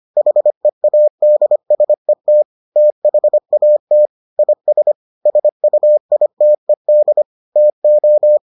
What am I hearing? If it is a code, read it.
Morse code, heard as a keyed tone, about 25 wpm, 600 Hz. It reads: HEADSET THAT IS SUITED TO